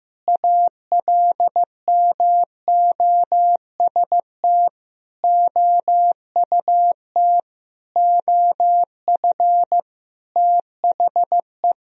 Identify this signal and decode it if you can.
Morse code, heard as a keyed tone, about 15 wpm, 700 Hz. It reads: ALMOST OUT OF THE
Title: Sentences from Top 1000 Words - Review Part 4 - 15wpm